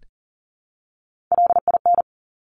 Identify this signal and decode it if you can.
Morse code, heard as a keyed tone, about 40 wpm, 700 Hz. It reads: LIN
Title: Top Three Letter Combinations in Words 40wpm